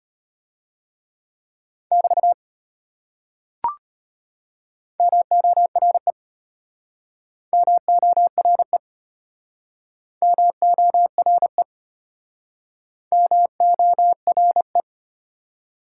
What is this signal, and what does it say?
Morse code, heard as a keyed tone, about 33 wpm, 700 Hz. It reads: <BT> MORE MORE MORE MORE